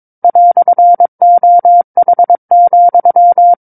Morse code, heard as a keyed tone, about 22 wpm, 700 Hz. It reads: "OH,
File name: 06-Chapter-06 22 WPM